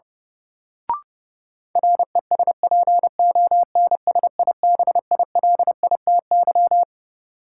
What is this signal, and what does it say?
Morse code, heard as a keyed tone, about 30 wpm, 700 Hz. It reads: RESPONSIBILITY
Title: Sentences from Top 1200 Words - New Word Drill Alt - 30wpm